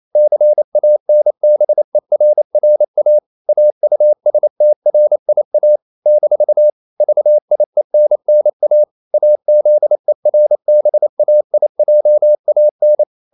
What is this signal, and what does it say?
Morse code, heard as a keyed tone, about 28 wpm, 600 Hz. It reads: CANBERRA AUSTRIA - VIENNA AZERBAIJAN
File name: CW / Practice / NationalCapitals28